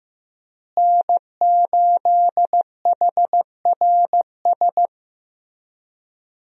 Morse code, heard as a keyed tone, about 15 wpm, 700 Hz. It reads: N8HRS
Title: USA Call Signs - 5 Characters - Repeated 3x - 15wpm